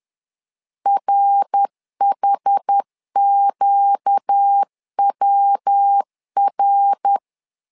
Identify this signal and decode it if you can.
Morse code, heard as a keyed tone, about 10 wpm, 800 Hz. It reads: RHQWR
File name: LEZIONI / 4 Q 10wpm 2210